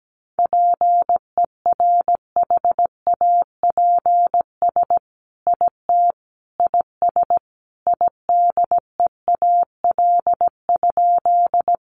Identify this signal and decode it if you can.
Morse code, heard as a keyed tone, about 17 wpm, 700 Hz. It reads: PERHAPS IT IS IDEAL?
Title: Sentences from Top 1400 Words 17wpm